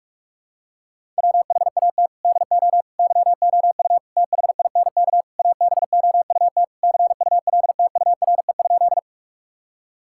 Morse code, heard as a keyed tone, about 41 wpm, 700 Hz. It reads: WHAT DO YOU THINK ABOUT CULTURE?
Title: Sentences from Top 1000 Words - Review Part 6 - 45wpm